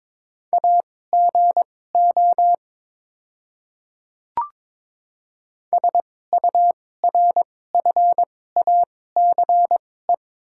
Morse code, heard as a keyed tone, about 22 wpm, spaced as for 16 wpm, 690 Hz. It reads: AGO SURFACE
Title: 2x Character Spacing - Top 500 Words - 22wpm